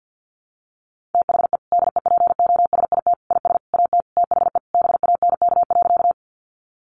Morse code, heard as a keyed tone, about 50 wpm, 700 Hz. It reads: THE DEPOSIT IS AT THE BANK.
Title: Sentences from Top 1300 Words 50wpm